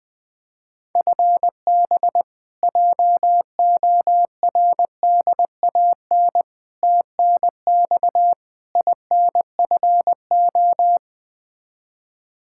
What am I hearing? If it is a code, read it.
Morse code, heard as a keyed tone, about 20 wpm, 700 Hz. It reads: FB JORDAN TNX INFO
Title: Practice QSO - Exchange 3 - More Info - 20wpm